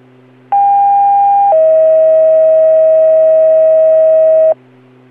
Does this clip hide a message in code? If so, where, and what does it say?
Yes: two-tone page from 0.5 to 4.5 s, heard as a one-second tone, then a three-second tone, about 800 then 640 Hz.